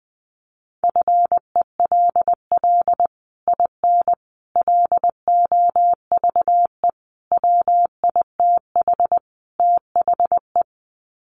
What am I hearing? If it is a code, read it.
Morse code, heard as a keyed tone, about 20 wpm, 700 Hz. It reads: FELL IN LOVE WITH THE
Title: Sets of 5 Words Encore2 20wpm